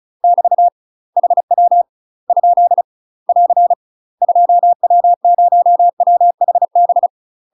Morse code, heard as a keyed tone, about 36 wpm, 700 Hz. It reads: <BT> HW ? <AR> 2W0WHB
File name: UK MASTER QSOs TRAINING / 35WPM / UK MASTER QSOs 0028